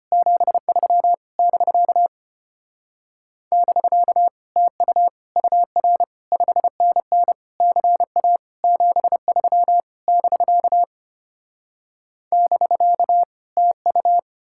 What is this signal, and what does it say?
Morse code, heard as a keyed tone, about 33 wpm, 700 Hz. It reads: 73 <BK> <BK> TU UR 5NN CA 73 <BK> <BK> TU